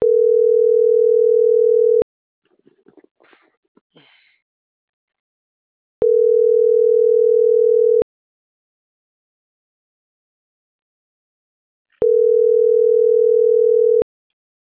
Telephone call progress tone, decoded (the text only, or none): ringback tone